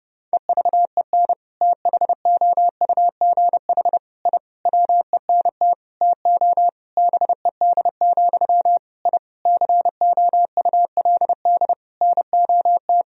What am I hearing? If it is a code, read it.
Morse code, heard as a keyed tone, about 31 wpm, 700 Hz. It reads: EVEN THOUGH I WENT TO BED, I COULD NOT
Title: Sentences from Top 1000 Words - Review Part 2 - 30wpm